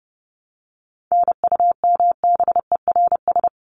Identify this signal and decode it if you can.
Morse code, heard as a keyed tone, about 30 wpm, 700 Hz. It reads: NUMBERS
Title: ES:S Suffix - Top 100 Words with Base Words 30wpm